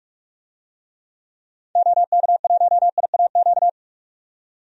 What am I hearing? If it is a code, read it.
Morse code, heard as a keyed tone, about 48 wpm, 700 Hz. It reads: OK1IAQ
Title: DXCC-100 Call Signs - 6 Characters - 45wpm